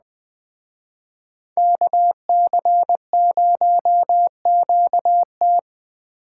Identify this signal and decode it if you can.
Morse code, heard as a keyed tone, about 20 wpm, 700 Hz. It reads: KC0QT